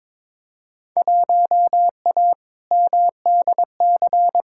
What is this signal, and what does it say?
Morse code, heard as a keyed tone, about 22 wpm, 700 Hz. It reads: 1A MDC
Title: Field Day Exchanges - 22wpm